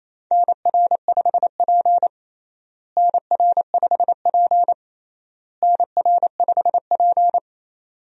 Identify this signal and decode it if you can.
Morse code, heard as a keyed tone, about 28 wpm, 700 Hz. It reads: NR5P NR5P NR5P
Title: USA Call Signs - 4 Characters - Repeated 3x - 28wpm